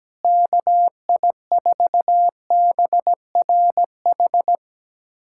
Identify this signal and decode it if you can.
Morse code, heard as a keyed tone, about 17 wpm, 700 Hz. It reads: KI4BRH
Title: USA Callsign DE Callsign 17wpm